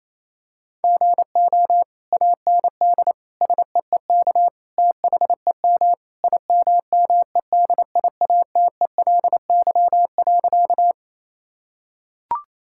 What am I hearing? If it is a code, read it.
Morse code, heard as a keyed tone, about 28 wpm, 700 Hz. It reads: GO AND SEEK THEM IMMEDIATELY.
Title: Sentences from Top 1600 Words 28wpm